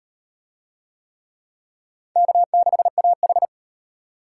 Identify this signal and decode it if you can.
Morse code, heard as a keyed tone, about 34 wpm, 700 Hz. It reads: K6AH